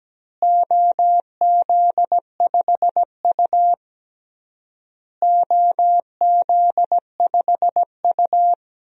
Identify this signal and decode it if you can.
Morse code, heard as a keyed tone, about 17 wpm, 700 Hz. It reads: OZ5U OZ5U
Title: DXCC-100 Call Signs - 4 Characters - Repeated 3x - 17wpm